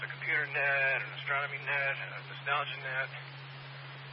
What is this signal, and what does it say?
Emergency Alert System (SAME) tones, heard as harsh buzzing bursts, with speech. It is heard from 0.6 to 3.1 s.